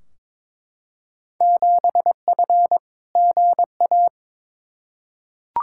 Morse code, heard as a keyed tone, about 22 wpm, 700 Hz. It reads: 7F GA
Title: Field Day Exchanges Encore - 22wpm